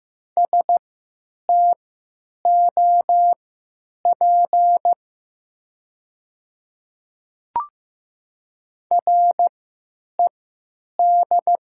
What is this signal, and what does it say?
Morse code, heard as a keyed tone, about 15 wpm, spaced as for 9 wpm, 700 Hz. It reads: STOP RED
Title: 3x Character Spacing - Top 500 Words - 15wpm